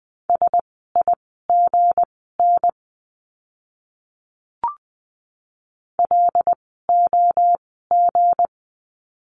Morse code, heard as a keyed tone, about 20 wpm, spaced as for 14 wpm, 700 Hz. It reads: SIGN LOG
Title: 160 Lesson 26f - Letter g - Words 20wpm